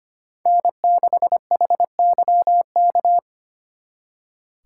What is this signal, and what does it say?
Morse code, heard as a keyed tone, about 30 wpm, 700 Hz. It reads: N6HYK
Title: Speed Racing - USA 1x3 Call Signs 25wpm